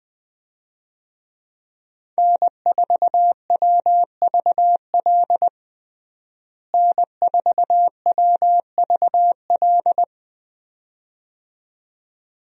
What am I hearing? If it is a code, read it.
Morse code, heard as a keyed tone, about 20 wpm, 700 Hz. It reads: N4WVL N4WVL
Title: USA Call Signs - 5 Characters - Repeated 2x - 20wpm